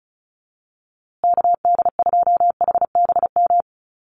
Morse code, heard as a keyed tone, about 34 wpm, 700 Hz. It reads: KD2HBM
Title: USA Call Signs Repeated 3x 35wpm